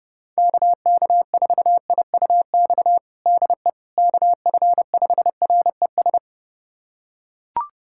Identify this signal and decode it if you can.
Morse code, heard as a keyed tone, about 30 wpm, 700 Hz. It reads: KK4IUX DE KF5RES
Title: USA Callsign DE Callsign 30wpm